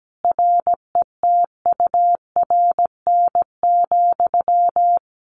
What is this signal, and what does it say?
Morse code, heard as a keyed tone, about 17 wpm, 700 Hz. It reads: RETURN,
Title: Sentences from Top 1500 Words 17wpm